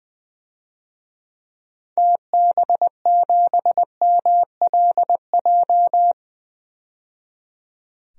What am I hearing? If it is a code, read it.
Morse code, heard as a keyed tone, about 21 wpm, 700 Hz. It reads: TB7MLJ
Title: DXCC-100 Call Signs - 6 Characters - 20wpm